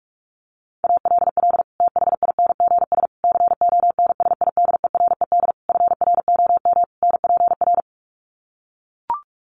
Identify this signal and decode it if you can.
Morse code, heard as a keyed tone, about 44 wpm, 700 Hz. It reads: ALL THINGS CONSIDERED FROM NPR
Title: Sets of 5 Words Encore 45wpm